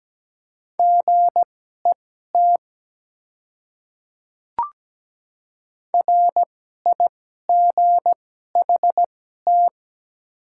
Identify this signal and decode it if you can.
Morse code, heard as a keyed tone, about 17 wpm, spaced as for 12 wpm, 700 Hz. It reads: GET RIGHT
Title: 2x Character Spacing - Top 300 Words - 17wpm